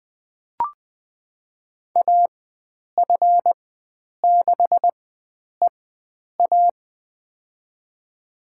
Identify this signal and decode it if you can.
Morse code, heard as a keyed tone, about 20 wpm, spaced as for 9 wpm, 700 Hz. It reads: AF6EA
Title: USA Call Signs - 5 Characters - 4x Character Spacing - 20wpm